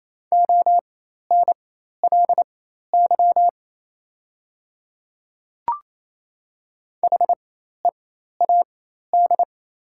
Morse code, heard as a keyed tone, about 28 wpm, spaced as for 13 wpm, 700 Hz. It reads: ONLY HEAD
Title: 4x Character Spacing - Top 300 Words - 28wpm